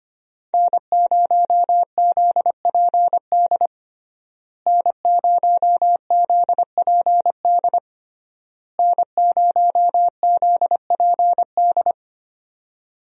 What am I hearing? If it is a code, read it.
Morse code, heard as a keyed tone, about 25 wpm, 700 Hz. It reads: N0ZPD N0ZPD N0ZPD
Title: USA Call Signs - 5 Characters - Repeated 3x - 25wpm